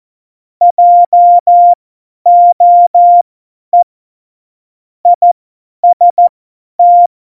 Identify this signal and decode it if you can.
Morse code, heard as a keyed tone, about 14 wpm, spaced as for 10 wpm, 700 Hz. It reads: JOE IST